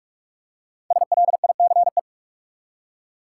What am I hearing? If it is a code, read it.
Morse code, heard as a keyed tone, about 46 wpm, 700 Hz. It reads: SLICE
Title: Five Letter Words - 45wpm